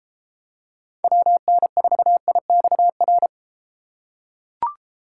Morse code, heard as a keyed tone, about 32 wpm, 700 Hz. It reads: WN4IXR